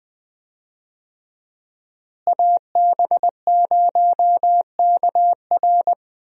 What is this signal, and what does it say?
Morse code, heard as a keyed tone, about 20 wpm, 700 Hz. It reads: AB0KR
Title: USA Call Signs - 5 Characters - Repeated 2x - 20wpm